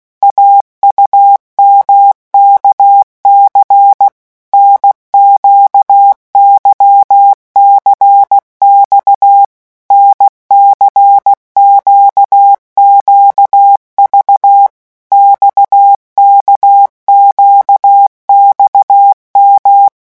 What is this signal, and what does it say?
Morse code, heard as a keyed tone, about 16 wpm, 800 Hz. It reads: AUMKC NQYCX NCQQV XKQXM
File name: morse mp3 / revision3-vitesse 15 mots